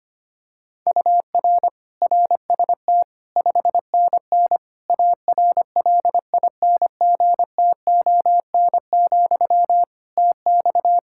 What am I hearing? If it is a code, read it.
Morse code, heard as a keyed tone, about 25 wpm, 700 Hz. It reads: UR RST 5NN ARLINGTON, TX
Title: Practice QSO - Exchange 1 - Beginning - 25wpm